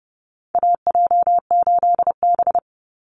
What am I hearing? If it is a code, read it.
Morse code, heard as a keyed tone, about 28 wpm, 700 Hz. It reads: AJ8B
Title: Call Signs Repeated 3x - 30wpm